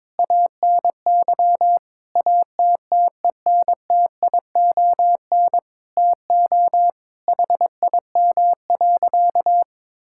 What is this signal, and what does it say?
Morse code, heard as a keyed tone, about 21 wpm, 700 Hz. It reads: ANY ATTENTION TO HIM.
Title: Sentences from Top 1000 Words - Review Part 6 - 22wpm